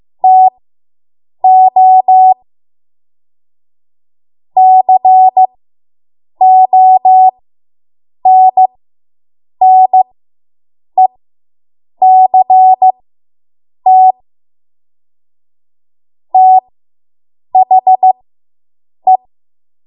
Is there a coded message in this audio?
Yes: Morse code from 0.2 to 19.2 s, heard as a keyed tone, about 15 wpm, spaced as for 7 wpm, 750 Hz. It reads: TO CONNECT THE